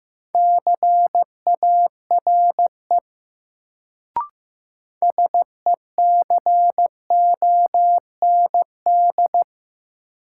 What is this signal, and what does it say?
Morse code, heard as a keyed tone, about 15 wpm, 700 Hz. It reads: CARE SECOND
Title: Sentences from Top 300 Words - New Word Drill - 15wpm